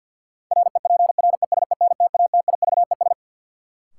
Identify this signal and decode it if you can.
Morse code, heard as a keyed tone, about 49 wpm, 700 Hz. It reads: REPRESENTATIVES
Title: Sets of 5 Words Encore - New Word Drill - 50wpm